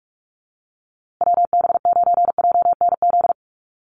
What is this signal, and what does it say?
Morse code, heard as a keyed tone, about 44 wpm, 700 Hz. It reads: WB9JNZ
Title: USA Call Signs Repeated 2x 45wpm